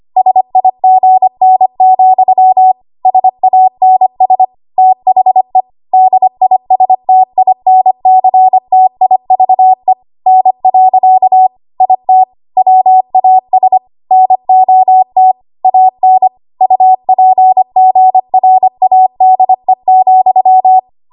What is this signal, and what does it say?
Morse code, heard as a keyed tone, about 25 wpm, 750 Hz. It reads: SIGN, SANS THE DISTINCTIVE N. IT WAS NOT AN UPGRADE,
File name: Archive / 25 WPM / 25WPM